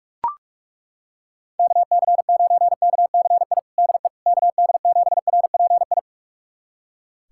Morse code, heard as a keyed tone, about 41 wpm, 700 Hz. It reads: KC9KCI DE KD7RPI
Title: USA Callsign DE Callsign 45wpm